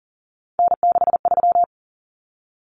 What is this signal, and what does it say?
Morse code, heard as a keyed tone, about 40 wpm, 700 Hz. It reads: N63
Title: Common Cut Numbers - 3-Digit 40wpm